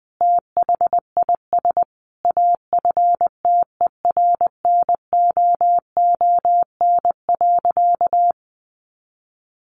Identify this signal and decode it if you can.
Morse code, heard as a keyed tone, about 20 wpm, 700 Hz. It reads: THIS AFTERNOON.
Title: Sentences from Top 1500 Words 20wpm